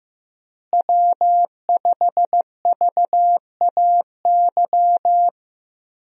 Morse code, heard as a keyed tone, about 15 wpm, 700 Hz. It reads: W5VAY
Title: USA Call Signs - 5 Characters - Repeated 2x - 15wpm